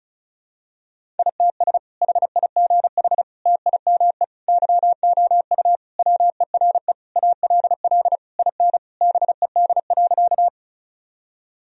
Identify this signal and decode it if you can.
Morse code, heard as a keyed tone, about 36 wpm, 700 Hz. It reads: ITS HIGH TIME YOU WERE ALL IN BED.
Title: Sentences from Top 1000 Words - Review Part 1 - 35wpm